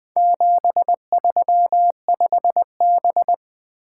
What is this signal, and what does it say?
Morse code, heard as a keyed tone, about 20 wpm, 700 Hz. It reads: 735B
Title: Four Letter-Number Combinations - 20wpm